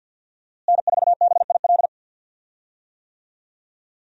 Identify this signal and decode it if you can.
Morse code, heard as a keyed tone, about 51 wpm, 700 Hz. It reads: N4BIL
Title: USA Call Signs - 5 Characters - Repeated 2x - 50wpm